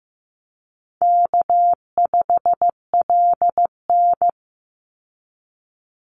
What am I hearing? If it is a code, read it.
Morse code, heard as a keyed tone, about 15 wpm, 700 Hz. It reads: K5LN
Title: Easy Call Signs - 15wpm